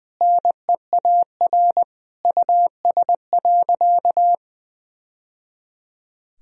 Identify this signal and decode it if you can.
Morse code, heard as a keyed tone, about 20 wpm, 700 Hz. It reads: NEAR US.
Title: Sentences from Top 1000 Words - Review Part 2 - 20wpm